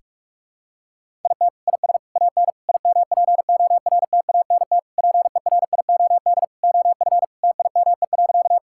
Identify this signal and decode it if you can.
Morse code, heard as a keyed tone, about 46 wpm, 700 Hz. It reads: IT IS AN IMPORTANT PERIOD OF TIME.
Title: Sentences from Top 1000 Words - Review Part 5 - 45wpm